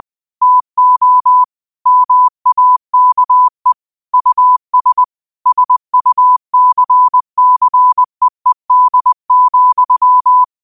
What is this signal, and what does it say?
Morse code, heard as a keyed tone, about 20 wpm, 1000 Hz. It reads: TO MAKE US SUCCEED,
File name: qotdfiles / 20wpm